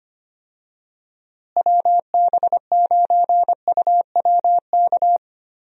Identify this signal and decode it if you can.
Morse code, heard as a keyed tone, about 25 wpm, 700 Hz. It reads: WB9UWK
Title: USA Call Signs - 6 Characters - Repeated 2x - 25wpm